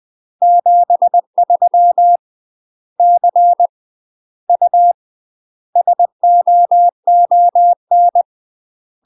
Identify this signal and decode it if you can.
Morse code, heard as a keyed tone, about 20 wpm, 700 Hz. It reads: 73 C U SOON
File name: UK MASTER QSOs TRAINING / 20WPM / UK MASTER QSOs 0049